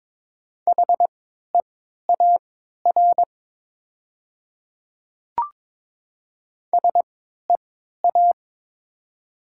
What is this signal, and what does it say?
Morse code, heard as a keyed tone, about 22 wpm, spaced as for 12 wpm, 700 Hz. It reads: HEAR SEA
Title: 3x Character Spacing - Top 300 Words - 22wpm